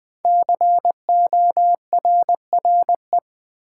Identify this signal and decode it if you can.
Morse code, heard as a keyed tone, about 20 wpm, 700 Hz. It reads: CORRE
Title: Build A Word - 5 Letters At A Time - 10-Letter Words - 20wpm